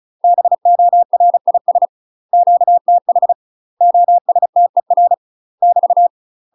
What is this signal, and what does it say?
Morse code, heard as a keyed tone, about 35 wpm, 700 Hz. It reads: BORIS QTH OSTER <BT>
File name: UK MASTER QSOs TRAINING / 35WPM / UK MASTER QSOs 0038